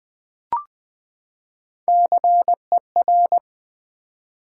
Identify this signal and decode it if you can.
Morse code, heard as a keyed tone, about 21 wpm, 700 Hz. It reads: CER
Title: Build A Word - 3 Letters At A Time - 9-Letter Words - 20wpm